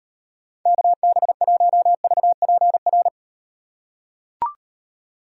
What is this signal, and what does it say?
Morse code, heard as a keyed tone, about 38 wpm, 700 Hz. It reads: KB1VPR